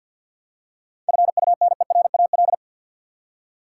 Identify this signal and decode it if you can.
Morse code, heard as a keyed tone, about 53 wpm, 700 Hz. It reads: FUNERAL
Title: Seven Letter Words - 50wpm